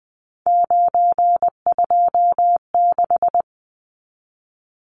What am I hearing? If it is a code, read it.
Morse code, heard as a keyed tone, about 20 wpm, 700 Hz. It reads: 926
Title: Numbers - 3-Digit 20wpm